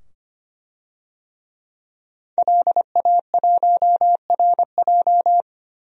Morse code, heard as a keyed tone, about 25 wpm, 700 Hz. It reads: LA1RJ
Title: DXCC-100 Call Signs - 5 Characters - 25wpm